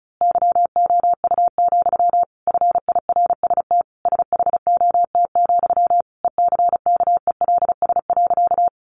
Morse code, heard as a keyed tone, about 34 wpm, 700 Hz. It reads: YOU, FIRST SHOT, ECKELS.
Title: A Sound Of Thunder 35wpm